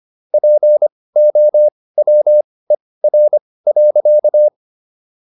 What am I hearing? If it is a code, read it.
Morse code, heard as a keyed tone, about 25 wpm, spaced as for 18 wpm, 600 Hz. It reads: POWER.